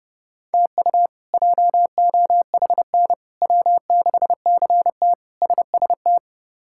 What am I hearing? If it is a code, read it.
Morse code, heard as a keyed tone, about 30 wpm, 700 Hz. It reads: TU JOHN W6CT SST
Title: K1USN SST QSO Practice - 30wpm